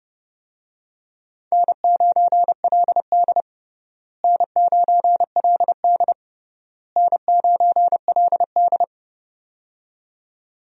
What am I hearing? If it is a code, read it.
Morse code, heard as a keyed tone, about 30 wpm, 700 Hz. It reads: N9LD N9LD N9LD
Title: USA Call Signs - 4 Characters - Repeated 3x - 30wpm